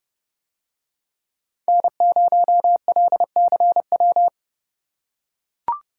Morse code, heard as a keyed tone, about 29 wpm, 700 Hz. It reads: N0LCW